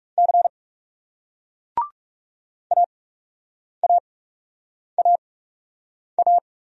Morse code, heard as a keyed tone, about 40 wpm, spaced as for 8 wpm, 700 Hz. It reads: C AAAA